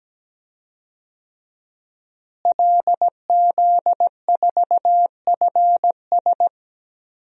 Morse code, heard as a keyed tone, about 17 wpm, 700 Hz. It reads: LZ4FS
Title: DXCC-100 Call Signs - 5 Characters - Repeated 3x - 17wpm